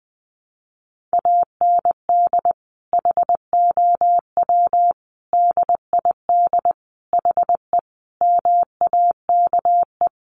Morse code, heard as a keyed tone, about 20 wpm, 700 Hz. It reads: AND HOW DID HE MAKE
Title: Sentences from Top 400 Words 20wpm